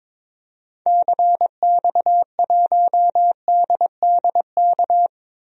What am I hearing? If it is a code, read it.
Morse code, heard as a keyed tone, about 22 wpm, 700 Hz. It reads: CX1DDK
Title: DXCC-100 Call Signs - 6 Characters - Repeated 3x - 22wpm